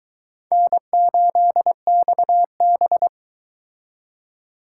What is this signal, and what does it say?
Morse code, heard as a keyed tone, about 22 wpm, 700 Hz. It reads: N8XB